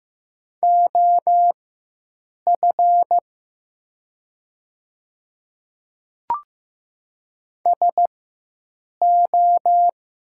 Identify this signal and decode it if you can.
Morse code, heard as a keyed tone, about 15 wpm, spaced as for 7 wpm, 700 Hz. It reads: OF SO
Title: 4x Character Spacing - Top 300 Words - 15wpm